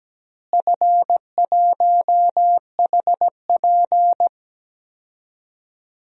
Morse code, heard as a keyed tone, about 17 wpm, 700 Hz. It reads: F1HP
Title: DXCC-100 Call Signs - 4 Characters - Repeated 3x - 17wpm